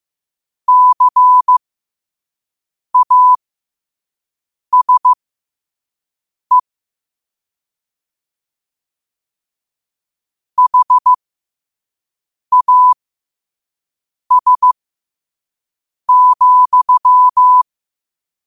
Morse code, heard as a keyed tone, about 15 wpm, spaced as for 5 wpm, 1000 Hz. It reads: CASE HAS,
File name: qotdfiles / 5wpm